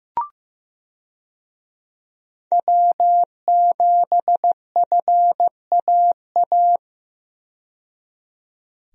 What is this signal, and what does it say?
Morse code, heard as a keyed tone, about 15 wpm, 700 Hz. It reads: W7FAA
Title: USA Call Signs - 5 Characters - 15wpm